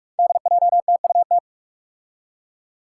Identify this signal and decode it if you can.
Morse code, heard as a keyed tone, about 46 wpm, 700 Hz. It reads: DJTUT
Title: Five Challenging Letter-Number Combinations 45wpm